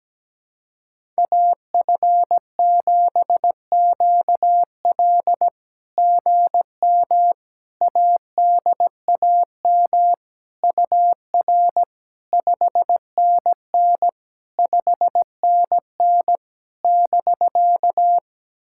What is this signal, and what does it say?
Morse code, heard as a keyed tone, about 17 wpm, 700 Hz. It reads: AF7QL GM ADAM UR 5NN 5NN <BK>
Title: SOTA QSO Practice 17wpm